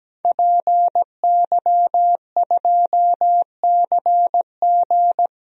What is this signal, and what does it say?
Morse code, heard as a keyed tone, about 17 wpm, 700 Hz. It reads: PY2CG
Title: DXCC-100 Call Signs - 5 Characters - 17wpm